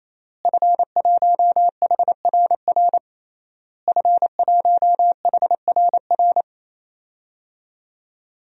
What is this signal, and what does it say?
Morse code, heard as a keyed tone, about 28 wpm, 700 Hz. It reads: F1HRR F1HRR
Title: DXCC-100 Call Signs - 5 Characters - Repeated 3x - 28wpm